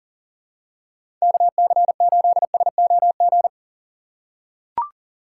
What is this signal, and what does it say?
Morse code, heard as a keyed tone, about 37 wpm, 700 Hz. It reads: KC8SOG
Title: USA Call Signs - 2x Character Spacing - 40wpm